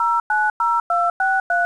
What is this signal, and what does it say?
DTMF, heard as telephone touch tones, tones of about 230 ms. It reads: 090263